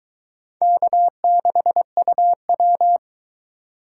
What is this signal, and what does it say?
Morse code, heard as a keyed tone, about 23 wpm, 700 Hz. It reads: K6UW